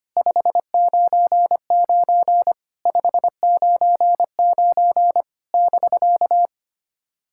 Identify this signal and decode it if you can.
Morse code, heard as a keyed tone, about 25 wpm, 700 Hz. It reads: 599 599 <BK>
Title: SOTA QSO Practice 25wpm